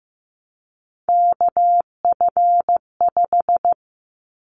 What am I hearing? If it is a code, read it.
Morse code, heard as a keyed tone, about 15 wpm, 700 Hz. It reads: KF5
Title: Callsign Prefixes for Top 25 DXCC - 15wpm